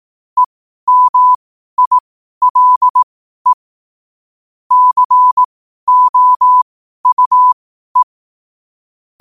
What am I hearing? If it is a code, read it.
Morse code, heard as a keyed tone, about 18 wpm, spaced as for 12 wpm, 1000 Hz. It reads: EMILE COUE
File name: qotdfiles / 12wpm